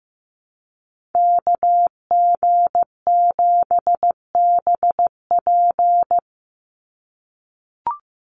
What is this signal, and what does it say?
Morse code, heard as a keyed tone, about 15 wpm, 700 Hz. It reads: KG7BP
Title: USA Call Signs 15wpm